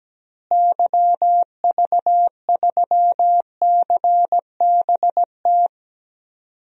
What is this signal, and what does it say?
Morse code, heard as a keyed tone, about 17 wpm, 700 Hz. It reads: YV3CBT
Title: DXCC-100 Call Signs - 6 Characters - Repeated 3x - 17wpm